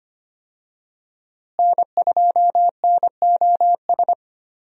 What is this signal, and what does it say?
Morse code, heard as a keyed tone, about 25 wpm, 700 Hz. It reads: N2NOS
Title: USA Call Signs - 5 Characters - Repeated 3x - 25wpm